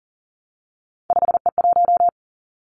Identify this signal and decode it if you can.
Morse code, heard as a keyed tone, about 42 wpm, 700 Hz. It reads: 5E1
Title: Three Challenging Letter-Number Combinations 40wpm